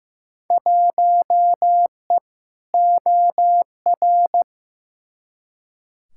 Morse code, heard as a keyed tone, about 15 wpm, 700 Hz. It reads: 1E OR
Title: Field Day Exchanges - 15wpm